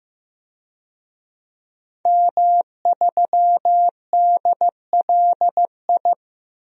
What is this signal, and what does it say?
Morse code, heard as a keyed tone, about 15 wpm, 700 Hz. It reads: M3DLI
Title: DXCC-100 Call Signs - 5 Characters - 15wpm